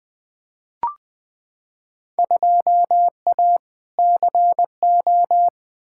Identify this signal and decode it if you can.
Morse code, heard as a keyed tone, about 20 wpm, 700 Hz. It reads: 2A CO
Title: Field Day Exchanges - 20wpm